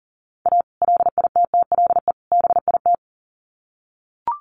Morse code, heard as a keyed tone, about 40 wpm, 700 Hz. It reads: A LITTLE BIT
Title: Sets of 3 Words 40wpm